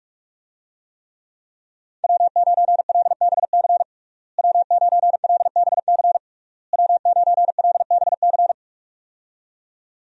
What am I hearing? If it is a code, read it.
Morse code, heard as a keyed tone, about 46 wpm, 700 Hz. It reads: W9LBC W9LBC W9LBC
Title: USA Call Signs - 5 Characters - Repeated 3x - 45wpm